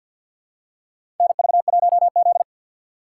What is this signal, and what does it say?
Morse code, heard as a keyed tone, about 53 wpm, 700 Hz. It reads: NV1Z
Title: USA Call Signs - 4 Characters - 6x Character Spacing - 50wpm